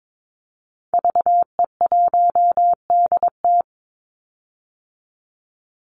Morse code, heard as a keyed tone, about 22 wpm, 700 Hz. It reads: VE1DT
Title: Call Signs - 22wpm